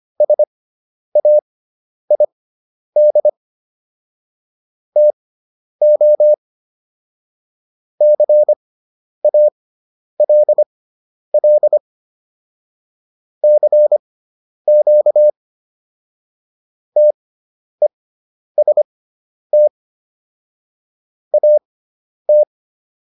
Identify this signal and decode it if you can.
Morse code, heard as a keyed tone, about 25 wpm, spaced as for 10 wpm, 600 Hz. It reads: SAID TO CALL CQ TEST AT